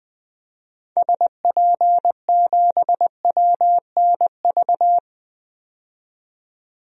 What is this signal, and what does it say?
Morse code, heard as a keyed tone, about 21 wpm, 700 Hz. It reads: SP7WNV
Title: DXCC-100 Call Signs - 6 Characters - 20wpm